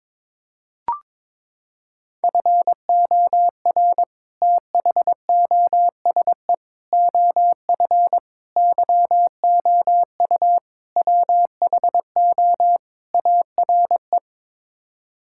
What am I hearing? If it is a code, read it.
Morse code, heard as a keyed tone, about 22 wpm, 700 Hz. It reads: FOR THOSE OF YOU WHO ARE
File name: practice / 22wpm / Sets of 6 Words - 22wpm